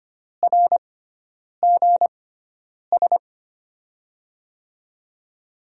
Morse code, heard as a keyed tone, about 25 wpm, spaced as for 9 wpm, 700 Hz. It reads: RGS
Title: USA Call Signs - 5 Characters - 6x Character Spacing - 25wpm